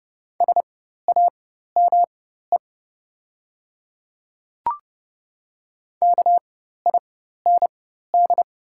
Morse code, heard as a keyed tone, about 30 wpm, spaced as for 14 wpm, 700 Hz. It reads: SAME KIND
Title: 4x Character Spacing - Top 300 Words - 30wpm